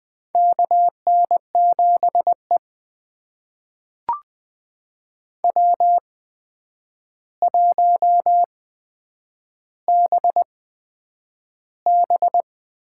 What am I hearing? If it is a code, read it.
Morse code, heard as a keyed tone, about 20 wpm, spaced as for 9 wpm, 700 Hz. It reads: KN7E W1BB
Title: USA Call Signs - 4 Characters - 8x Character Spacing - 20wpm